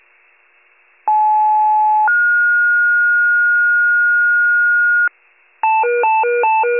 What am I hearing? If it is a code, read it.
Two-tone page, heard as a one-second tone, then a three-second tone, about 860 then 1430 Hz, with noise. It is heard from 1.1 to 5.1 s.
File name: wcalert